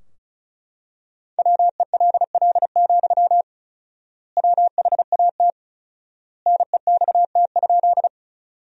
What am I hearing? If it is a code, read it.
Morse code, heard as a keyed tone, about 36 wpm, 700 Hz. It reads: WELL, WHAT NEXT?
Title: 4x Word-Spacing - Sentences from Top 300 Words - 35wpm